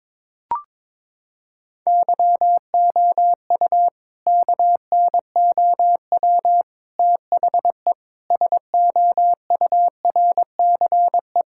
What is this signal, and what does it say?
Morse code, heard as a keyed tone, about 22 wpm, 700 Hz. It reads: YOU KNOW THE SOURCE
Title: Sentences from Top 1000 Words - Review Part 5 - 22wpm